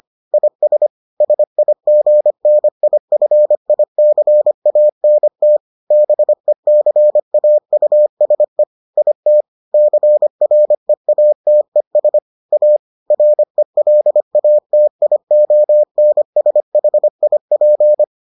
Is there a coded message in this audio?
Yes: Morse code from 0.3 to 18.0 s, heard as a keyed tone, about 25 wpm, 600 Hz. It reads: IS SIGNIFICANT BECAUSE IT CREATES A RELATIONSHIP